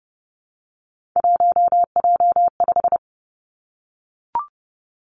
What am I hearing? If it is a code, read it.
Morse code, heard as a keyed tone, about 32 wpm, 700 Hz. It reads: 1J5
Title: Three Challenging Letter-Number Combinations 30wpm